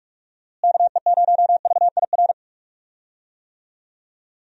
Morse code, heard as a keyed tone, about 42 wpm, 700 Hz. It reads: KE0VIR
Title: USA Call Signs - 6 Characters - Repeated 3x - 45wpm